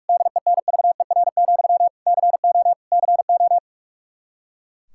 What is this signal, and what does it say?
Morse code, heard as a keyed tone, about 41 wpm, 700 Hz. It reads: DENVER, CO CO
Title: Practice QSO - Exchange 1 - Beginning - 45wpm